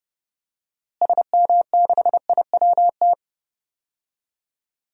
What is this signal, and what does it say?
Morse code, heard as a keyed tone, about 30 wpm, 700 Hz. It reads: SM6IWT
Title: DXCC-100 Call Signs - 6 Characters - Repeated 2x - 30wpm